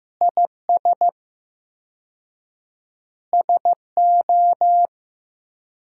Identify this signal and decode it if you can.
Morse code, heard as a keyed tone, about 15 wpm, 700 Hz. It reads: IS SO
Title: 4x Word-Spacing - Sentences from Top 500 Words - 15wpm